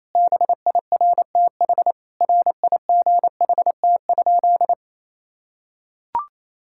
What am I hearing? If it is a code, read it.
Morse code, heard as a keyed tone, about 28 wpm, 700 Hz. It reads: BIRTH RIGHT?
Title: Sentences from Top 1000 Words - Review Part 5 - 28wpm